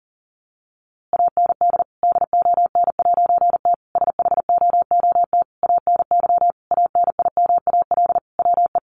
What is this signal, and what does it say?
Morse code, heard as a keyed tone, about 40 wpm, 700 Hz. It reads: AND DON'T SHOOT ANY ANIMAL WE
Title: A Sound Of Thunder 40wpm